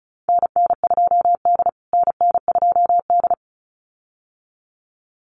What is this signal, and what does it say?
Morse code, heard as a keyed tone, about 34 wpm, 700 Hz. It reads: NN2D NN2D
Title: USA Call Signs Repeated 2x 35wpm